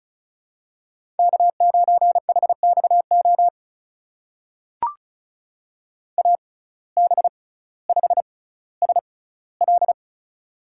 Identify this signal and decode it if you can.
Morse code, heard as a keyed tone, about 36 wpm, spaced as for 18 wpm, 700 Hz. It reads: K9HXO AB5SL
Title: USA Call Signs - 5 Characters - 6x Character Spacing - 35wpm